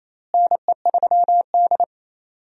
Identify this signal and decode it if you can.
Morse code, heard as a keyed tone, about 28 wpm, 700 Hz. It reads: NE3D
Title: USA Call Signs - 4 Characters - Repeated 2x - 28wpm